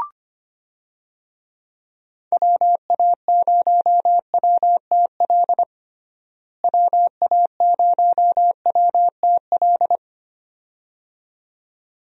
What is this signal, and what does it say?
Morse code, heard as a keyed tone, about 25 wpm, 700 Hz. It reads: WA0WTL WA0WTL
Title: USA Call Signs - 6 Characters - Repeated 2x - 25wpm